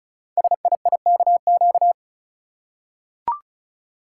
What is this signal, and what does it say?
Morse code, heard as a keyed tone, about 36 wpm, 700 Hz. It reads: SIIKQ
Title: Code Groups - 5 Letters - 35wpm